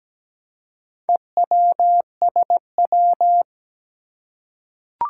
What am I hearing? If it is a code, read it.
Morse code, heard as a keyed tone, about 17 wpm, 700 Hz. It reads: EWSW
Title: Four Challenging Letter-Number Combinations 17wpm